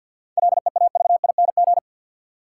Morse code, heard as a keyed tone, about 51 wpm, 700 Hz. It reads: LEAVING
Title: Sentences from Top 1000 Words - Review Part 4 - New Word Drill - 50wpm